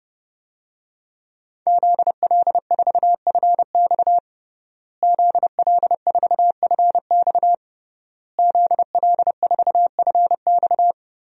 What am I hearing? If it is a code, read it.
Morse code, heard as a keyed tone, about 31 wpm, 700 Hz. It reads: ZL4FX ZL4FX ZL4FX
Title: DXCC-100 Call Signs - 5 Characters - Repeated 3x - 30wpm